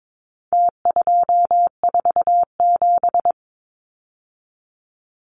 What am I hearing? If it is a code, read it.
Morse code, heard as a keyed tone, about 22 wpm, 700 Hz. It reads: T247
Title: Numbers - Contest Style - 4-Digit 22wpm